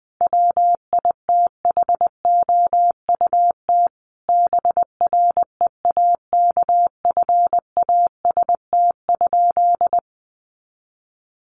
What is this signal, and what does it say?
Morse code, heard as a keyed tone, about 20 wpm, 700 Hz. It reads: WITHOUT BREAKFAST?
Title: Sentences from Top 1500 Words 20wpm